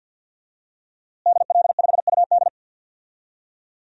Morse code, heard as a keyed tone, about 51 wpm, 700 Hz. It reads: DL5UD
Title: DXCC-100 Call Signs - 5 Characters - Repeated 2x - 50wpm